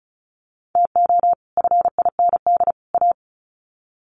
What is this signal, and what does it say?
Morse code, heard as a keyed tone, about 34 wpm, 700 Hz. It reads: TO FIND A
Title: Sets of 3 Words 35wpm